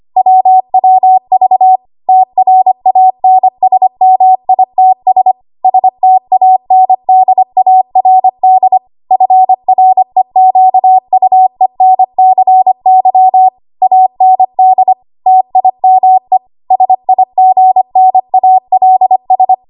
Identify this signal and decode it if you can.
Morse code, heard as a keyed tone, about 25 wpm, 750 Hz. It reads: WWV TRANSMITS STANDARD FREQUENCY AND TIME SIGNALS
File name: Archive / 25 WPM / 25WPM